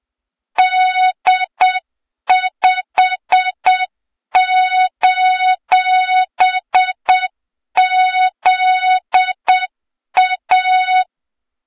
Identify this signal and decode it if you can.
Morse code, heard as a keyed tone, about 8 wpm, 3700 Hz, with noise. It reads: D5:ZA
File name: Morze-HA2MN / hang / 23 GYAKORLAT